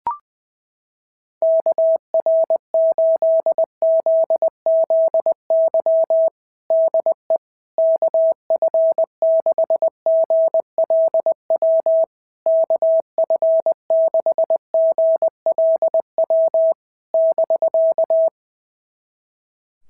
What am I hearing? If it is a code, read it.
Morse code, heard as a keyed tone, about 19 wpm, 650 Hz. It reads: KR8ZZY DE KF6GLW KF6GLW <BK>
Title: SOTA QSO Practice 20wpm